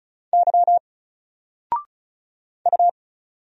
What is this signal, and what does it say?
Morse code, heard as a keyed tone, about 34 wpm, 700 Hz. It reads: Y U
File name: Single-Letter-Number-35wpm